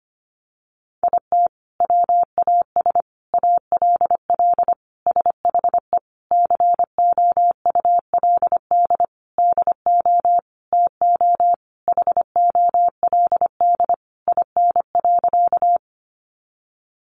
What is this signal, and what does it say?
Morse code, heard as a keyed tone, about 25 wpm, 700 Hz. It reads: IT WAS ALL SHE COULD DO TO HOLD IN.
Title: Sentences from Top 400 Words 25wpm